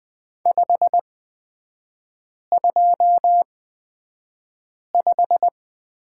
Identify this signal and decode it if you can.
Morse code, heard as a keyed tone, about 20 wpm, spaced as for 5 wpm, 700 Hz. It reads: 525
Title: 250 Lesson 40e - BK - Three Characters 20wpm